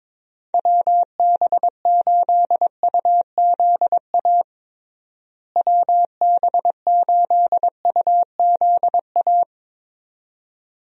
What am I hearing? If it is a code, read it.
Morse code, heard as a keyed tone, about 22 wpm, 700 Hz. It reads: WB8UZA WB8UZA
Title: USA Call Signs - 6 Characters - Repeated 2x - 22wpm